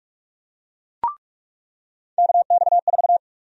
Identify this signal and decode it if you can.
Morse code, heard as a keyed tone, about 46 wpm, 700 Hz. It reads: KX4
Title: USA Call Sign Prefixes - 45wpm